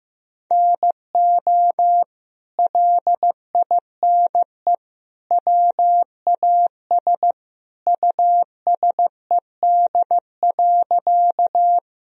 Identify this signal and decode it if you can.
Morse code, heard as a keyed tone, about 14 wpm, 700 Hz. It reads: NO LINE WAS USED.
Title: Sentences from Top 1000 Words - Review Part 1 - 15wpm